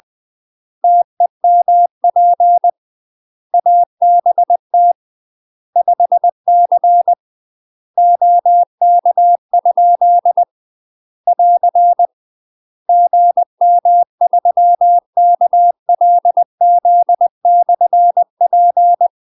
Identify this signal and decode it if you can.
Morse code, heard as a keyed tone, about 20 wpm, 700 Hz. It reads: TEMP ABT 5C OK? <AR> GM3KLZ/P
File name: UK MASTER QSOs TRAINING / 20WPM / UK MASTER QSOs 0044